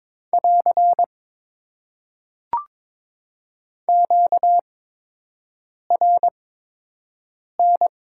Morse code, heard as a keyed tone, about 22 wpm, spaced as for 6 wpm, 700 Hz. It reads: <AR> QRN
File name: Top 90 QSO Elements - 8x Character Spacing - 22wpm